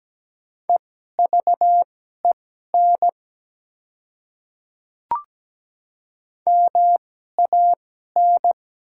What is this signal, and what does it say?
Morse code, heard as a keyed tone, about 17 wpm, spaced as for 12 wpm, 700 Hz. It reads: EVEN MAN
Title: 2x Character Spacing - Top 200 Words - 17wpm